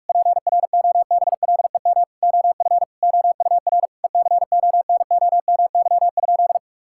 Morse code, heard as a keyed tone, about 41 wpm, 700 Hz. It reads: PROBLEM OF OUR ECONOMY?
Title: Sentences from Top 1000 Words - Review Part 5 - 45wpm